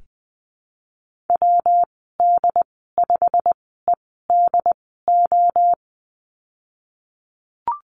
Morse code, heard as a keyed tone, about 20 wpm, spaced as for 15 wpm, 700 Hz. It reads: WD5EDO
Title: 219 Lesson 35g - Number 0 - Call Signs 20wpm